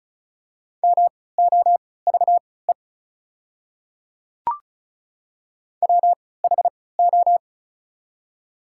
Morse code, heard as a keyed tone, about 36 wpm, spaced as for 20 wpm, 700 Hz. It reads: MOVE WHO
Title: 3x Character Spacing - Top 200 Words - 35wpm